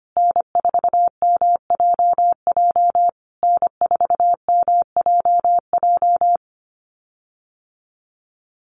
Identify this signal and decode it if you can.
Morse code, heard as a keyed tone, about 25 wpm, 700 Hz. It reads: N4MJJ N4MJJ
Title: USA Call Signs Repeated 3x 25wpm